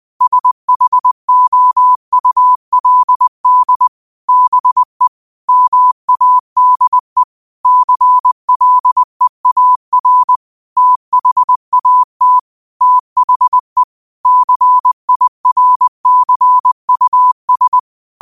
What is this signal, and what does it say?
Morse code, heard as a keyed tone, about 20 wpm, 1000 Hz. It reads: SHOULD BE MADE CLEAR THAT THE CIRCUS
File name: qotdfiles / 20wpm